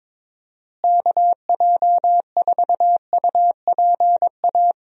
Morse code, heard as a keyed tone, about 22 wpm, 700 Hz. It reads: KJ4UPA
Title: USA Call Signs - 6 Characters - Repeated 2x - 22wpm